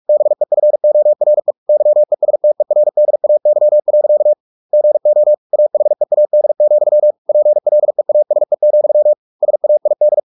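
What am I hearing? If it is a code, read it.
Morse code, heard as a keyed tone, about 44 wpm, 600 Hz. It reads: BEFORE YESTERDAY. GO AHEAD, PLEASE, SAID
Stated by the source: Morse practice at 45 wpm, the novel Never by Ken Follett in English, from CW med Gnister.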